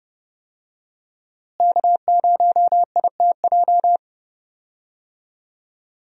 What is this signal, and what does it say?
Morse code, heard as a keyed tone, about 32 wpm, 700 Hz. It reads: K0ITJ
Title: USA Call Signs - 5 Characters - Repeated 3x - 30wpm